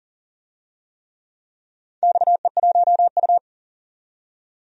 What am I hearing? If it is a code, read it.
Morse code, heard as a keyed tone, about 38 wpm, 690 Hz. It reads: XE1U
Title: DXCC-100 Call Signs - 4 Characters - 40wpm